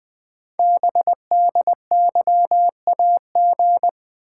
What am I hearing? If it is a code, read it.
Morse code, heard as a keyed tone, about 20 wpm, 700 Hz. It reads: BDYAG
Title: Code Groups - 5 Letters - 20wpm